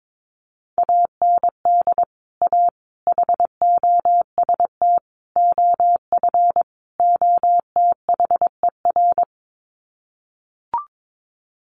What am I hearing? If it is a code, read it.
Morse code, heard as a keyed tone, about 22 wpm, 700 Hz. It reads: AND A HOST OF OTHER
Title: Sets of 5 Words Encore2 22wpm